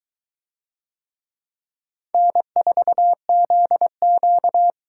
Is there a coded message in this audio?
Yes: Morse code from 2.1 to 4.7 s, heard as a keyed tone, about 22 wpm, 700 Hz. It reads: N4ZQ